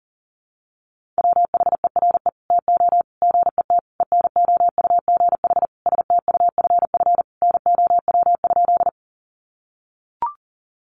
Morse code, heard as a keyed tone, about 40 wpm, 700 Hz. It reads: WHERE TO GET ENOUGH STUFF NOW?
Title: Sentences from Top 1400 Words 40wpm